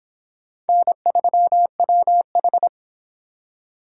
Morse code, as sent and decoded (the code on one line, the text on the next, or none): -. ...-- .-- ....
N3WH